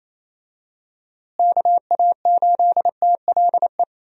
Morse code, heard as a keyed tone, about 28 wpm, 700 Hz. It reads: KA8TLE
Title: USA Call Signs - 6 Characters - Repeated 2x - 28wpm